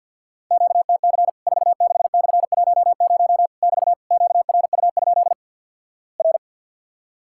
Morse code, heard as a keyed tone, about 50 wpm, 700 Hz. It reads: QTC 46/10 <BT> QRU? R
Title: Worked All Europe DX Contest - QTC Exchange with Repeats - 50wpm